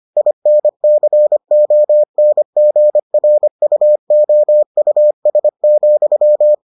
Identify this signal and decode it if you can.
Morse code, heard as a keyed tone, about 25 wpm, 600 Hz. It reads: INCONGRUOUS,